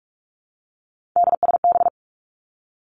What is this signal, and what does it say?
Morse code, heard as a keyed tone, about 46 wpm, 700 Hz. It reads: DSB
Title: Three Challenging Letter-Number Combinations 45wpm